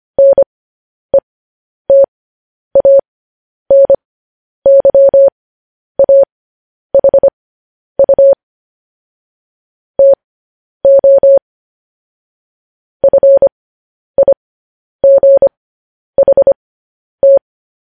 Morse code, heard as a keyed tone, about 25 wpm, spaced as for 10 wpm, 550 Hz. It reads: NETANYAHU TO FIGHT
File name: podcast / News headlines-1x-25x10